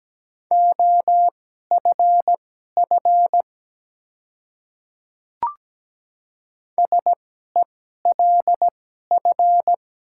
Morse code, heard as a keyed tone, about 17 wpm, spaced as for 12 wpm, 700 Hz. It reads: OFF SELF
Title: 2x Character Spacing - Top 300 Words - 17wpm